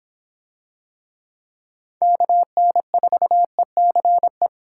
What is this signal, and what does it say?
Morse code, heard as a keyed tone, about 26 wpm, 700 Hz. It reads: KN4ECE